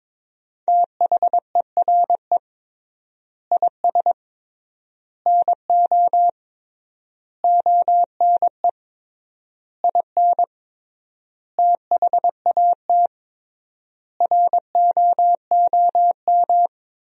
Morse code, heard as a keyed tone, about 22 wpm, 700 Hz. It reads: THERE IS NO ONE IN THAT ROOM
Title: 3x Word-Spacing - Sentences from Top 500 Words - 22wpm